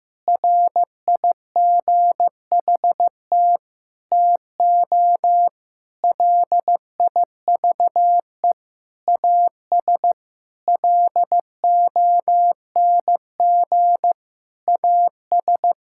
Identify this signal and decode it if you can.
Morse code, heard as a keyed tone, about 15 wpm, 700 Hz. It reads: RIGHT TO LIVE AS LONG AS
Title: Sentences from Top 1000 Words - Review Part 5 - 15wpm